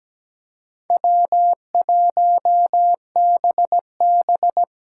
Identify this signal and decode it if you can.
Morse code, heard as a keyed tone, about 17 wpm, 700 Hz. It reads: W1BB
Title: USA Call Signs - 4 Characters - 4x Character Spacing - 17wpm